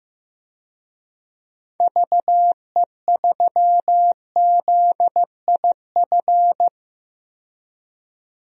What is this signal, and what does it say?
Morse code, heard as a keyed tone, about 15 wpm, 700 Hz. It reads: VE3ZIF
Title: DXCC-100 Call Signs - 6 Characters - 15wpm